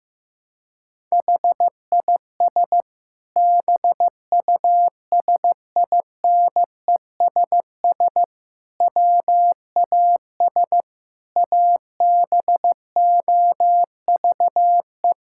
Morse code, heard as a keyed tone, about 15 wpm, 700 Hz. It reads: HIS BUSINESS WAS ABOVE
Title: Sentences from Top 1000 Words - Review Part 6 - 15wpm